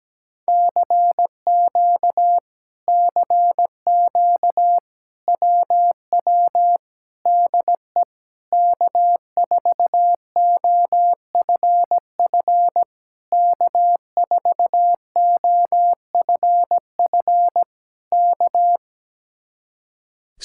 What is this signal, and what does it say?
Morse code, heard as a keyed tone, about 17 wpm, 700 Hz. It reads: CQ CQ WW DE K4OFF K4OFF K
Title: Speed-Racing Common QSO Phrases Encore - 17wpm